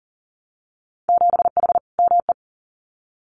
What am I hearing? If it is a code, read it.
Morse code, heard as a keyed tone, about 40 wpm, 700 Hz. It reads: 7H ME
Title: Winter Field Day Exchanges Encore 40wpm